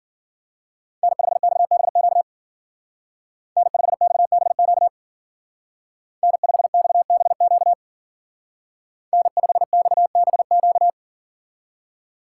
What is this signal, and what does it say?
Morse code, heard as a keyed tone, about 50 wpm, 700 Hz. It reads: N5XBQ N5XBQ N5XBQ N5XBQ
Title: Speed Racing - USA Call Signs 40wpm